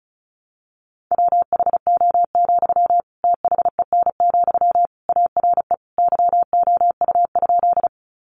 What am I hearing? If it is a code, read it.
Morse code, heard as a keyed tone, about 34 wpm, 700 Hz. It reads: WHO, THEN, ARE YOU?
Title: Sentences from Top 100 Words 35wpm